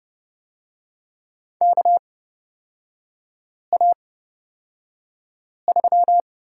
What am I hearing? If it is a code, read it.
Morse code, heard as a keyed tone, about 29 wpm, spaced as for 5 wpm, 700 Hz. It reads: KA3
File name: Lesson 38e - Letter q - Three Characters 30wpm